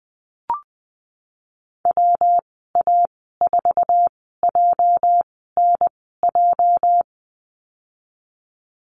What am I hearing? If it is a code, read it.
Morse code, heard as a keyed tone, about 20 wpm, spaced as for 15 wpm, 700 Hz. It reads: WA4JNJ
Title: 213 Lesson 34g - Number 8 - Call Signs 20wpm